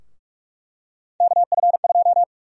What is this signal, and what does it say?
Morse code, heard as a keyed tone, about 46 wpm, 700 Hz. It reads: KF2
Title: USA Call Sign Prefixes - 45wpm